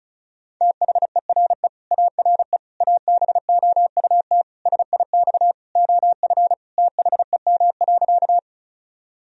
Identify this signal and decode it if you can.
Morse code, heard as a keyed tone, about 34 wpm, 700 Hz. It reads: THERE ARE ABOUT SIX OF THEM.
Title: Sentences from Top 1000 Words - Review Part 2 - 35wpm